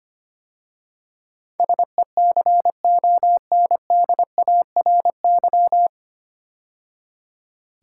Morse code, heard as a keyed tone, about 25 wpm, 690 Hz. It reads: SECONDARY
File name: practice / 25wpm / 9 Letter Words - First 3 Letters Given - 25wpm